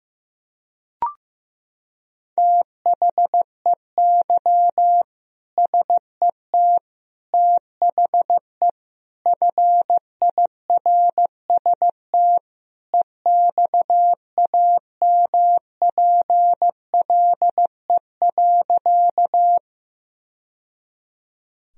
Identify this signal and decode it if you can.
Morse code, heard as a keyed tone, about 15 wpm, 700 Hz. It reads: THEY SET THE FIRST EXAMPLE.
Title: Sentences from Top 1000 Words - Review Part 1 - 15wpm